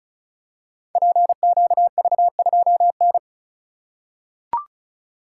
Morse code, heard as a keyed tone, about 36 wpm, 700 Hz. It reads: PQV2N
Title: Code Groups - 5 Letter:Numbers - 35wpm